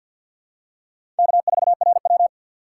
Morse code, heard as a keyed tone, about 52 wpm, 690 Hz. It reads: K4RW
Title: USA Call Signs - 4 Characters - 6x Character Spacing - 50wpm